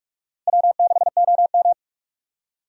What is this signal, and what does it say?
Morse code, heard as a keyed tone, about 46 wpm, 700 Hz. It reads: W6OM
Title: USA Call Signs - 4 Characters - 45wpm